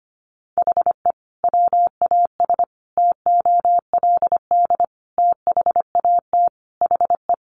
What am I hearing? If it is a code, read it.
Morse code, heard as a keyed tone, about 25 wpm, 700 Hz. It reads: HE WAS TOLD THAT HE
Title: Sentences from Top 1500 Words 25wpm